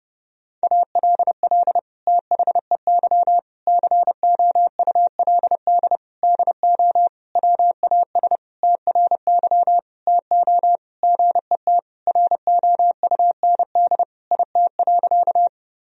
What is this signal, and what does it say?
Morse code, heard as a keyed tone, about 30 wpm, 700 Hz. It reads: ALL THEY COULD DO WAS TRY TO GET ROUND IT.
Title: Sentences from Top 1000 Words - Review Part 1 - 30wpm